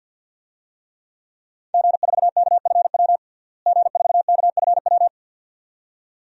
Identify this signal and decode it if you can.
Morse code, heard as a keyed tone, about 50 wpm, 700 Hz. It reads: G4KFW G4KFW
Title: DXCC-100 Call Signs - 5 Characters - Repeated 2x - 50wpm